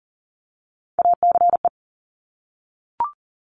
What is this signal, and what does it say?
Morse code, heard as a keyed tone, about 41 wpm, 690 Hz. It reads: ACE
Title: Top Three Letter Combinations in Words 40wpm